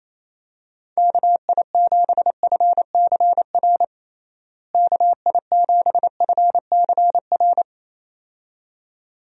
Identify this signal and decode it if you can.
Morse code, heard as a keyed tone, about 28 wpm, 700 Hz. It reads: KI7FCR KI7FCR
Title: USA Call Signs - 6 Characters - Repeated 2x - 28wpm